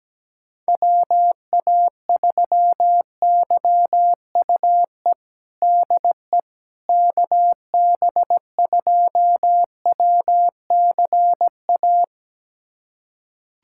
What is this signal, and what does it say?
Morse code, heard as a keyed tone, about 17 wpm, 700 Hz. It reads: WA3YUE DE KB2WCA
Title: USA Callsign DE Callsign 17wpm